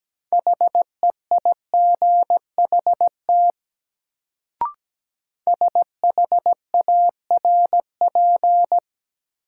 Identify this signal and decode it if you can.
Morse code, heard as a keyed tone, about 17 wpm, 700 Hz. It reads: HEIGHT SHARP
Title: Sentences from Top 1600 Words - New Word Drill - 17wpm